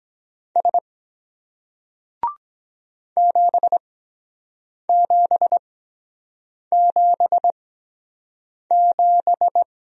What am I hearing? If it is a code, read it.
Morse code, heard as a keyed tone, about 23 wpm, spaced as for 7 wpm, 700 Hz. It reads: S 7777